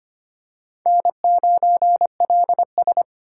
Morse code, heard as a keyed tone, about 25 wpm, 700 Hz. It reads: N9LS
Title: USA Call Signs - 4 Characters - 2x Character Spacing - 25wpm